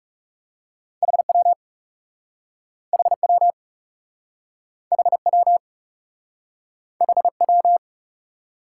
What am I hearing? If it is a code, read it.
Morse code, heard as a keyed tone, about 37 wpm, 700 Hz. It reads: HW HW HW HW
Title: Speed-Racing Top 90 QSO Elements 30wpm